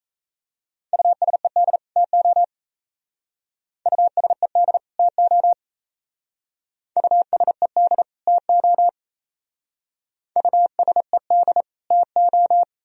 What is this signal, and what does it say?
Morse code, heard as a keyed tone, about 34 wpm, 700 Hz. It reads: USED TO USED TO USED TO USED TO
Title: Speed-Racing - Sets of 2 Words 28wpm